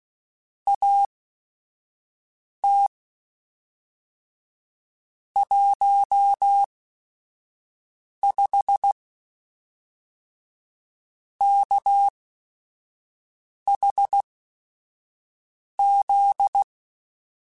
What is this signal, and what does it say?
Morse code, heard as a keyed tone, about 16 wpm, spaced as for 4 wpm, 800 Hz. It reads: AT 15KHZ
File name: Technical Subjects / Technical Subjects-mp3s / Sw Pwr Sup @5